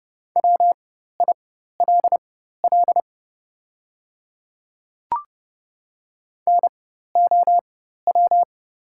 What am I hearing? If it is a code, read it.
Morse code, heard as a keyed tone, about 30 wpm, spaced as for 14 wpm, 700 Hz. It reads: WILL NOW
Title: 4x Character Spacing - Top 100 Words - 30wpm